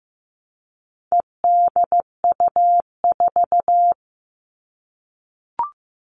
Morse code, heard as a keyed tone, about 15 wpm, 700 Hz. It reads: EDU4
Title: All Cut Numbers - 4-Digit 15wpm